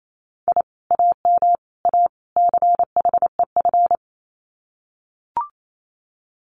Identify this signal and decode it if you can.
Morse code, heard as a keyed tone, about 28 wpm, 700 Hz. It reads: I AM A CHEF
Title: Common QSO Phrases Encore 28wpm